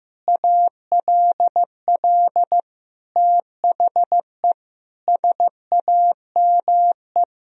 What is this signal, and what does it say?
Morse code, heard as a keyed tone, about 15 wpm, 700 Hz. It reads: ALL THE SAME
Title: Sentences from Top 1000 Words - Review Part 4 - 15wpm